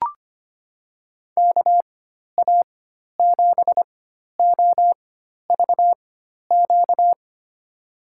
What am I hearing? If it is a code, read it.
Morse code, heard as a keyed tone, about 25 wpm, spaced as for 12 wpm, 700 Hz. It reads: KA7OVQ
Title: USA Call Signs - 6 Characters - 4x Character Spacing - 25wpm